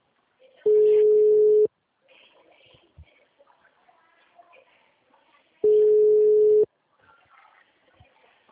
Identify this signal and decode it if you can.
Ringback tone (Europe, 425 Hz).